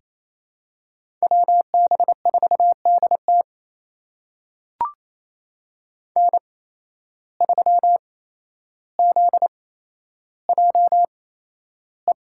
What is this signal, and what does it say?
Morse code, heard as a keyed tone, about 28 wpm, spaced as for 12 wpm, 700 Hz. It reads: WB4DT N3ZJE
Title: USA Call Signs - 8x Character Spacing - 28wpm